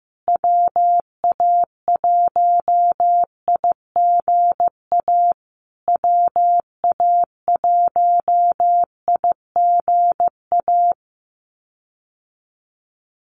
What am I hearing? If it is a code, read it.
Morse code, heard as a keyed tone, about 15 wpm, 700 Hz. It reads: WA1IGA WA1IGA
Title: USA Call Signs Repeated 3x 15wpm